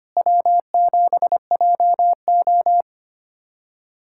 Morse code, heard as a keyed tone, about 25 wpm, 700 Hz. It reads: W7JO
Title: USA Callsign DE Callsign - 4 Characters - 25wpm